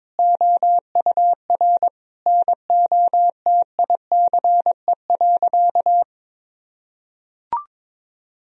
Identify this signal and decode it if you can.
Morse code, heard as a keyed tone, about 21 wpm, 700 Hz. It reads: OUR NOTICE.
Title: Sentences from Top 1000 Words - Review Part 2 - 22wpm